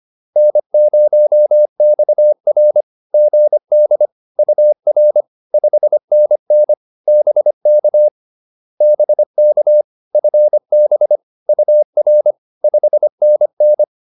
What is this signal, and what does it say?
Morse code, heard as a keyed tone, about 25 wpm, 600 Hz. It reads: N0XR GD UR 5NN BK BK FB UR 5NN
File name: POTA107 25